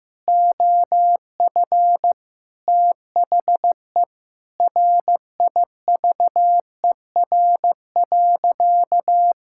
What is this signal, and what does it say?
Morse code, heard as a keyed tone, about 15 wpm, 700 Hz. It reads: OF THE RIVER.
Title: Sentences from Top 1000 Words - Review Part 5 - 15wpm